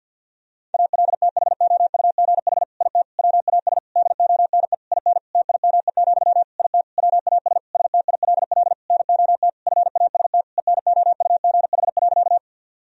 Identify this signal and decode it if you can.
Morse code, heard as a keyed tone, about 50 wpm, 700 Hz. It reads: ALTHOUGH IT WAS DONE IN TIME, IT WAS STILL NOT FAST ENOUGH.
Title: Sentences from Top 1000 Words - Review Part 3 - 50wpm